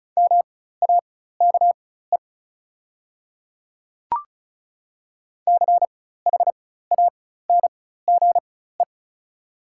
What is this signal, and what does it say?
Morse code, heard as a keyed tone, about 36 wpm, spaced as for 16 wpm, 700 Hz. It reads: MAKE CHANGE
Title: 4x Character Spacing - Top 200 Words - 35wpm